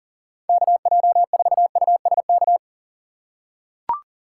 Morse code, heard as a keyed tone, about 37 wpm, 700 Hz. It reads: KJ4USK
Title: USA Call Signs - 6 Characters - 6x Character Spacing - 40wpm